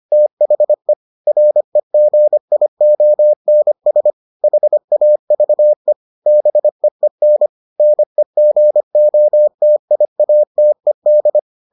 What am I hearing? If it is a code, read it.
Morse code, heard as a keyed tone, about 25 wpm, 600 Hz. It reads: THE REGIONS HAVE BEEN NEGOTIATED